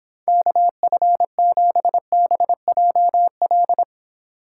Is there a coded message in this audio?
Yes: Morse code from 0.3 to 3.8 s, heard as a keyed tone, about 31 wpm, 700 Hz. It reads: KF7BJL